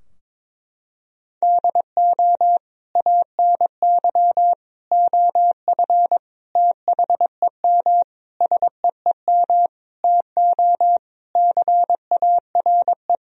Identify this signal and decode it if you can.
Morse code, heard as a keyed tone, about 22 wpm, 700 Hz. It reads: DO ANY OF THEM SEEM TO CARE
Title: Sentences from Top 1000 Words - Review Part 3 - 22wpm